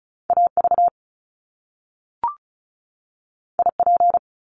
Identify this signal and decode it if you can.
Morse code, heard as a keyed tone, about 36 wpm, 700 Hz. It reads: AV IP
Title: Top Two Letter Combinations in Words 35wpm